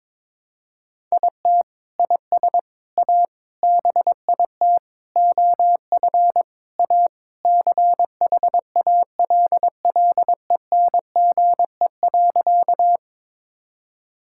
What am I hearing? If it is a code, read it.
Morse code, heard as a keyed tone, about 22 wpm, 700 Hz. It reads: IT IS A BIT OF A CHALLENGE.
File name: practice / 22wpm / Sentences with 4-Word Phrases - 22wpm